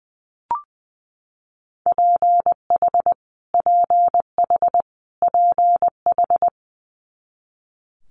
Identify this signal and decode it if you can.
Morse code, heard as a keyed tone, about 20 wpm, 700 Hz. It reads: PH PH PH
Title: Multi-letter Phonemes Repeated 3x 20wpm